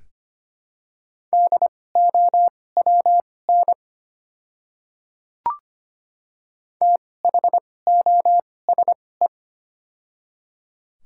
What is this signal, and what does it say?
Morse code, heard as a keyed tone, about 25 wpm, spaced as for 18 wpm, 700 Hz. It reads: DOWN THOSE
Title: 2x Character Spacing - Top 300 Words - 25wpm